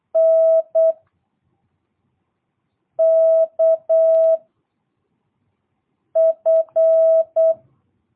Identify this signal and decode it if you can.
Morse code, heard as a keyed tone, about 8 wpm, 660 Hz. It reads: N K F
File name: jh8cbh / 2022cw